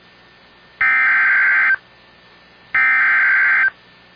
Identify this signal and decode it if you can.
Emergency Alert System (SAME) tones (header bursts), heard as harsh buzzing bursts. It is heard from 0.8 to 3.6 s.